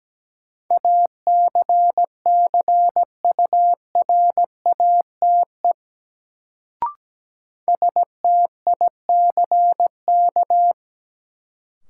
Morse code, heard as a keyed tone, about 17 wpm, 700 Hz. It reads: ACCURATE STICK
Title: Sentences from Top 1200 Words - New Word Drill - 17wpm